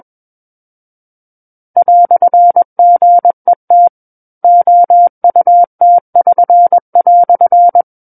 Morse code, heard as a keyed tone, about 21 wpm, 700 Hz. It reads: "GET OUT<SN>"
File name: Book One / 12-Chapter 12 - 21 WPM